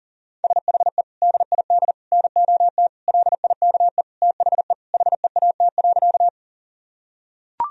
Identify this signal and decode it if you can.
Morse code, heard as a keyed tone, about 40 wpm, 700 Hz. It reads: SHE DID NOT LIKE THE HEAT.
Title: Sentences from Top 1000 Words - Review Part 3 - 40wpm